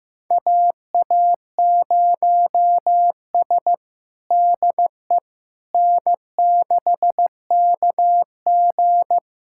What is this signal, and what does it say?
Morse code, heard as a keyed tone, about 16 wpm, 700 Hz. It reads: AA0S DE N6KG
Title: USA Callsign DE Callsign - 4 Characters - 15wpm